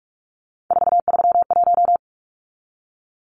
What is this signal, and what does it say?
Morse code, heard as a keyed tone, about 44 wpm, 700 Hz. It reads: 431
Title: Numbers - 3-Digit 45wpm